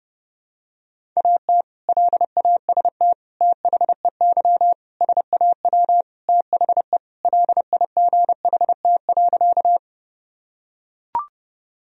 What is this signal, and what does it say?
Morse code, heard as a keyed tone, about 32 wpm, 700 Hz. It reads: AT LAST THEY SAW THE LIGHT.
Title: Sentences from Top 1000 Words - Review Part 1 - 30wpm